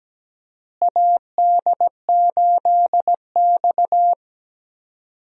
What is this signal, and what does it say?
Morse code, heard as a keyed tone, about 17 wpm, 700 Hz. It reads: AD8X
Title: USA Call Signs - 4 Characters - Repeated 3x - 17wpm